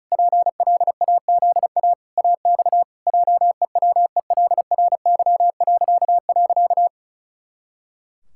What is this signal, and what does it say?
Morse code, heard as a keyed tone, about 34 wpm, 700 Hz. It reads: PLAZA AX JEWELRY..
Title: Pangram Challenge v2 - 35wpm